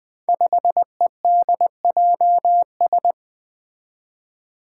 Morse code, heard as a keyed tone, about 20 wpm, 700 Hz. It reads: 5EDJS
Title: Five Challenging Letter-Number Combinations 20wpm